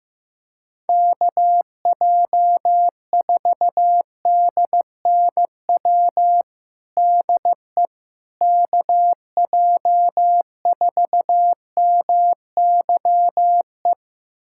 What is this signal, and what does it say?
Morse code, heard as a keyed tone, about 15 wpm, 700 Hz. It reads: KJ4DNW DE KJ4MYE
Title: USA Callsign DE Callsign - 6 Characters - 15wpm